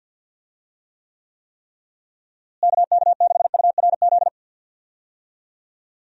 Morse code, heard as a keyed tone, about 50 wpm, 700 Hz. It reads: KK6URZ
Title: USA Call Signs - 6 Characters - Repeated 3x - 50wpm